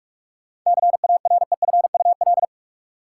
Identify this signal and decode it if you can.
Morse code, heard as a keyed tone, about 42 wpm, 700 Hz. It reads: CAREFUL
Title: Sentences from Top 1600 Words - New Word Drill Alt - 45wpm